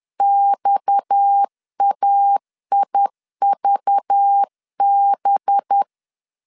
Morse code, heard as a keyed tone, about 10 wpm, 800 Hz. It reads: XAIVB
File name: LEZIONI / 6 PKX0 / 6 PKX 10wpm 2426